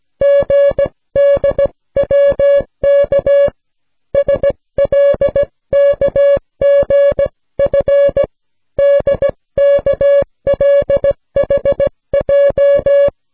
Morse code, heard as a keyed tone, about 16 wpm, 560 Hz. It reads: GDWK SLKGF DKLHJ